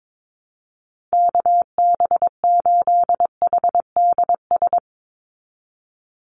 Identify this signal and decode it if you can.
Morse code, heard as a keyed tone, about 22 wpm, 700 Hz. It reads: KB8HDS
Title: USA Call Signs 22wpm